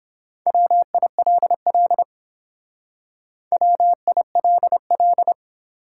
Morse code, heard as a keyed tone, about 26 wpm, 700 Hz. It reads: WILL WILL
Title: Speed-Racing Top 180 QSO Elements 22wpm